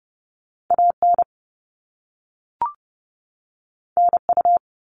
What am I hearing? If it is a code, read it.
Morse code, heard as a keyed tone, about 30 wpm, 700 Hz. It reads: AN NU
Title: All Cut Numbers - 2-Digit 30wpm